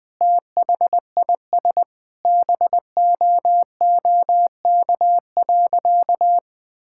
Morse code, heard as a keyed tone, about 21 wpm, 700 Hz. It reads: THIS BOOK.
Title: Sentences with 2-Word Phrases - 20wpm